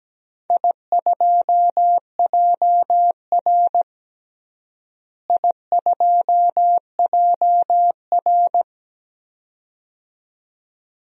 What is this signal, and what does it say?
Morse code, heard as a keyed tone, about 17 wpm, 700 Hz. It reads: I2JR I2JR
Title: DXCC-100 Call Signs - 4 Characters - Repeated 2x - 17wpm